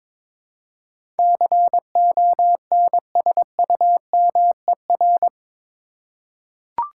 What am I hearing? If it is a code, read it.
Morse code, heard as a keyed tone, about 22 wpm, 700 Hz. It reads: CONSUMER
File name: practice / 22wpm / Eight Letter Words - 22wpm